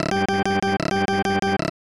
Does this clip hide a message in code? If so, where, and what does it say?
No.